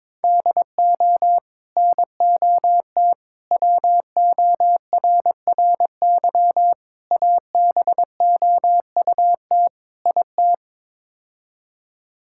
Morse code, heard as a keyed tone, about 22 wpm, 700 Hz. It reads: DO NOT WORRY ABOUT IT
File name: practice / 22wpm / Build A Phrase - 5-Word Phrases - 22wpm